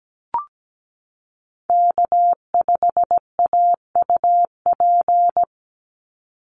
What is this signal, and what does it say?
Morse code, heard as a keyed tone, about 17 wpm, 700 Hz. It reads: K5AUP
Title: Call Signs Repeated 2x - 17wpm